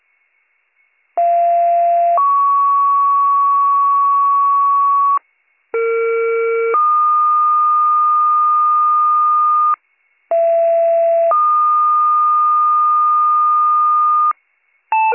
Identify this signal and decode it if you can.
Two-tone page, heard as a one-second tone, then a three-second tone, about 690 then 1120 Hz, about 460 then 1220 Hz, about 670 then 1220 Hz. It is heard from 1.2 to 14.3 s.